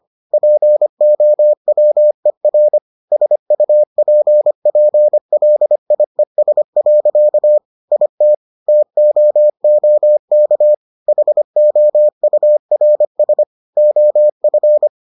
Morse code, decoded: POWER SUPPLIES. IT TOOK HOURS OF

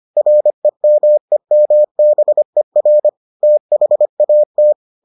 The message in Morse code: REMEMBER THAT

